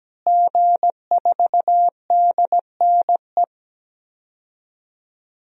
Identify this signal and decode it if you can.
Morse code, heard as a keyed tone, about 17 wpm, 700 Hz. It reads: G4DNE
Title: DXCC-100 Call Signs - 5 Characters - Repeated 3x - 17wpm